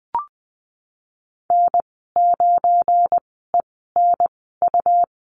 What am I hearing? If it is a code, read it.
Morse code, heard as a keyed tone, about 20 wpm, spaced as for 15 wpm, 700 Hz. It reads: N9ENU
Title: 186 Lesson 30g - Letter b - Call Signs 20wpm